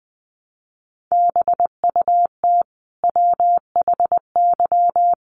Morse code, heard as a keyed tone, about 20 wpm, 700 Hz. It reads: BUT WHY
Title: Sentences from Top 1300 Words 20wpm